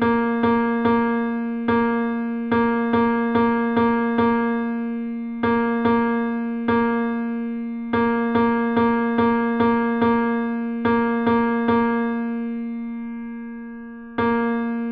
There is no code.